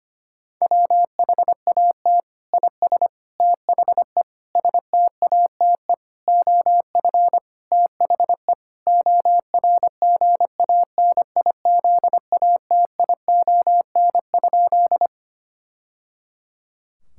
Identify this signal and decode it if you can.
Morse code, heard as a keyed tone, about 25 wpm, 700 Hz. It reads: WHAT IS THE STATE OF THE ORGANIZATION?
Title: Sentences from Top 1000 Words - Review Part 5 - 25wpm